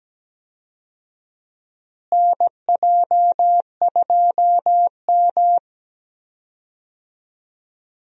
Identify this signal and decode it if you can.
Morse code, heard as a keyed tone, about 17 wpm, 700 Hz. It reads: NJ2M
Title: USA Call Signs - 4 Characters - Repeated 2x - 17wpm